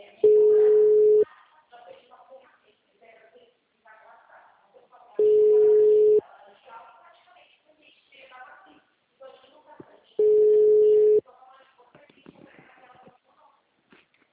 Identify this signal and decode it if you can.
Ringback tone (Europe, 425 Hz), with speech.